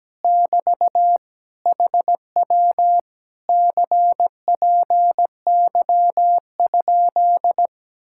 Morse code, heard as a keyed tone, about 17 wpm, 700 Hz. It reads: <BT> HW CPY?
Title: Speed-Racing Common QSO Phrases - 17wpm